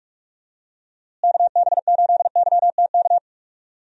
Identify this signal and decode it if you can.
Morse code, heard as a keyed tone, about 45 wpm, 700 Hz. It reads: KB8YTK
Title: USA Call Signs - 6 Characters - Repeated 2x - 45wpm